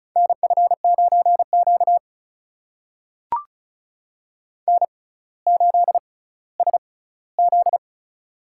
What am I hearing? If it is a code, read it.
Morse code, heard as a keyed tone, about 36 wpm, spaced as for 18 wpm, 700 Hz. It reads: NF9Q N8SZ
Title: USA Call Signs - 4 Characters - 6x Character Spacing - 35wpm